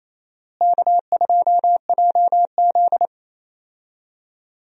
Morse code, heard as a keyed tone, about 27 wpm, 700 Hz. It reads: K2JZ
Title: USA Call Signs - 4 Characters - Repeated 3x - 28wpm